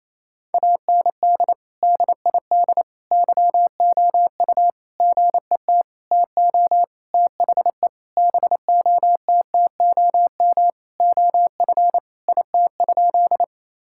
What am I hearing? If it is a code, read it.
Morse code, heard as a keyed tone, about 27 wpm, 700 Hz. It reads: AND DID YOU GET TO THE BOTTOM OF IT?
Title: Sentences from Top 1000 Words - Review Part 6 - 28wpm